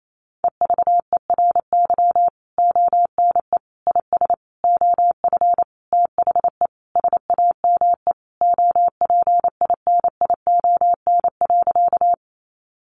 Morse code, heard as a keyed tone, about 28 wpm, 700 Hz. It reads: EVERY ONE IS OF THE SAME OPINION.
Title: Sentences from Top 1200 Words 28wpm